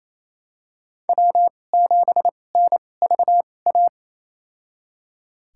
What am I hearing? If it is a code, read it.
Morse code, heard as a keyed tone, about 28 wpm, spaced as for 20 wpm, 700 Hz. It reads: W7NVA
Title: USA Call Signs - 5 Characters - 2x Character Spacing - 28wpm